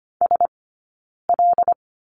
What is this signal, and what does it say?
Morse code, heard as a keyed tone, about 25 wpm, 700 Hz. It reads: S L
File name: Lesson 17d - Letter u - Two Characters 25wpm